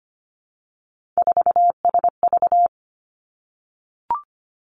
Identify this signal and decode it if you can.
Morse code, heard as a keyed tone, about 25 wpm, 700 Hz. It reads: 4SV
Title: Three Challenging Letter-Number Combinations 25wpm